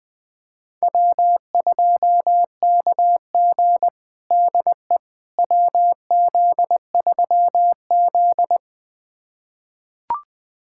Morse code, heard as a keyed tone, about 21 wpm, 700 Hz. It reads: W2KG DE WZ3Z
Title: USA Callsign DE Callsign - 4 Characters - 20wpm